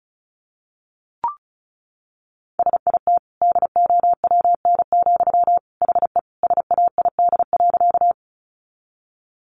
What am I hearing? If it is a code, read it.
Morse code, heard as a keyed tone, about 34 wpm, 700 Hz. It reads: SIT DOWN, HE SAID.
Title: Sentences from Top 500 Words 35wpm